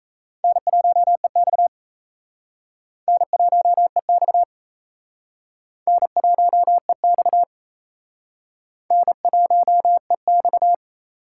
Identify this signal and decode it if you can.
Morse code, heard as a keyed tone, about 34 wpm, 700 Hz. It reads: N1EX N1EX N1EX N1EX
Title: Speed Racing - USA 1x2 Call Signs 28wpm